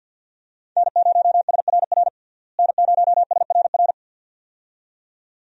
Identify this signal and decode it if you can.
Morse code, heard as a keyed tone, about 48 wpm, 700 Hz. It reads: N0SRR N0SRR
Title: USA Call Signs - 5 Characters - Repeated 2x - 50wpm